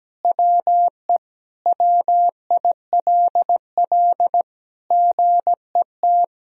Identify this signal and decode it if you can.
Morse code, heard as a keyed tone, about 17 wpm, 700 Hz. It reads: WE WILL GET
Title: Sentences from Top 1000 Words - Review Part 1 - 17wpm